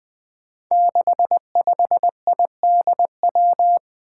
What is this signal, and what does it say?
Morse code, heard as a keyed tone, about 20 wpm, 700 Hz. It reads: 65IDW
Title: Five Challenging Letter-Number Combinations 20wpm